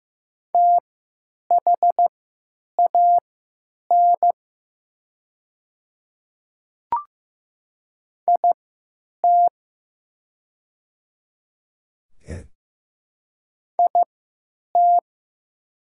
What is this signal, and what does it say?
Morse code, heard as a keyed tone, about 15 wpm, spaced as for 9 wpm, 700 Hz. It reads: THAN IT IT
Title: 3x Character Spacing - Top 100 Words - 15wpm